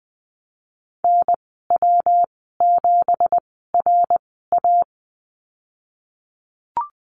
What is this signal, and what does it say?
Morse code, heard as a keyed tone, about 20 wpm, spaced as for 15 wpm, 700 Hz. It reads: NW7RA
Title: 167 Lesson 27g - Number 7 - Call Signs 20wpm